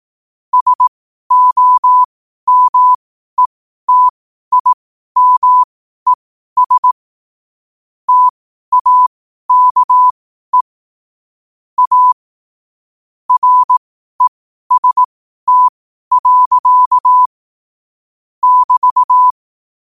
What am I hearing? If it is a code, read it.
Morse code, heard as a keyed tone, about 18 wpm, spaced as for 12 wpm, 1000 Hz. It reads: SOMETIMES TAKE A REST. <BT>
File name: qotdfiles / 12wpm